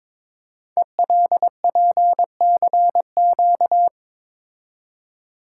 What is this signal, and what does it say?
Morse code, heard as a keyed tone, about 22 wpm, 700 Hz. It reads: ELPCQ
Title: Code Groups - 5 Letters - 22wpm